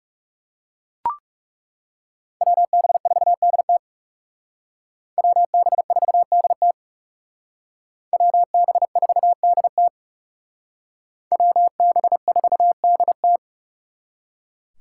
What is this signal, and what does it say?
Morse code, heard as a keyed tone, about 39 wpm, 700 Hz. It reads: WB4DT WB4DT WB4DT WB4DT